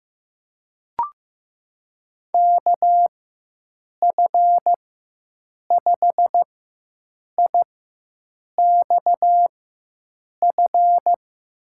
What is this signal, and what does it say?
Morse code, heard as a keyed tone, about 15 wpm, spaced as for 7 wpm, 700 Hz. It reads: KF5IXF
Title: USA Call Signs - 6 Characters - 4x Character Spacing - 15wpm